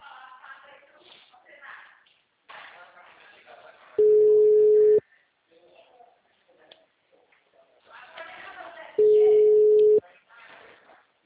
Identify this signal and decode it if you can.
Ringback tone (Europe, 425 Hz), with speech.